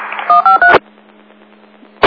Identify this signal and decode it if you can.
DTMF, heard as telephone touch tones, tones of about 130 ms. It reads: 153
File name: Inlet Fire 08 39 19